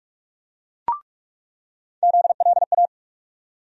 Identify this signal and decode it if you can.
Morse code, heard as a keyed tone, about 46 wpm, 700 Hz. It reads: ZLA
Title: USA Call Sign Postfixes - 3 Letters 45wpm